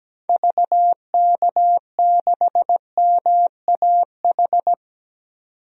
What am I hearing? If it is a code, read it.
Morse code, heard as a keyed tone, about 17 wpm, 700 Hz. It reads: VK6MAH
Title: DXCC-100 Call Signs - 6 Characters - 17wpm